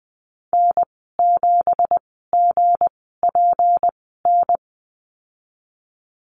Morse code, heard as a keyed tone, about 20 wpm, spaced as for 15 wpm, 700 Hz. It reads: N7GPN
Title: 200 Lesson 32g - Letter k - Call Signs 20wpm